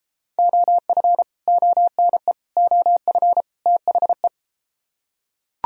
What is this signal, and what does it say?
Morse code, heard as a keyed tone, about 33 wpm, 700 Hz. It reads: OF ONE OF THE